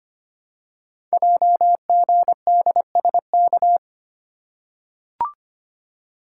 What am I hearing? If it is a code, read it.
Morse code, heard as a keyed tone, about 25 wpm, 700 Hz. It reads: JGDSK
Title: Code Groups - 5 Letter:Numbers - 25wpm